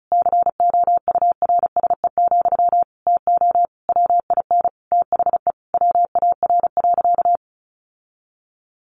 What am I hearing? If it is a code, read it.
Morse code, heard as a keyed tone, about 34 wpm, 700 Hz. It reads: COURSE, TO WIN THE WAR.
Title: Sentences from Top 1500 Words 35wpm